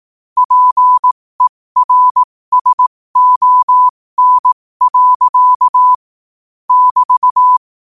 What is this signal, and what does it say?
Morse code, heard as a keyed tone, about 18 wpm, spaced as for 15 wpm, 1000 Hz. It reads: PERSON. <BT>
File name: qotdfiles / 15wpm